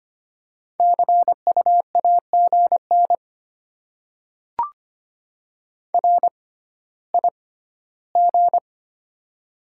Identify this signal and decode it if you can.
Morse code, heard as a keyed tone, about 25 wpm, spaced as for 14 wpm, 690 Hz. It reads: CU AGN RIG
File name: Top 20 QSO Elements - 6x Character Spacing - 25wpm